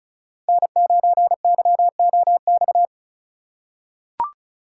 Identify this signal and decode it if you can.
Morse code, heard as a keyed tone, about 36 wpm, 700 Hz. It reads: N9YOX
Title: USA Call Signs - 5 Characters - 3x Character Spacing - 35wpm